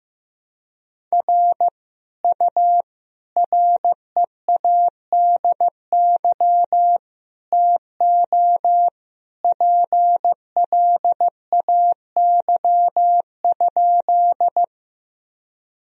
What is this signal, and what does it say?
Morse code, heard as a keyed tone, about 15 wpm, 700 Hz. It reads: R U READY TO PLAY?
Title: Sentences from Top 400 Words and Abbreviations 15wpm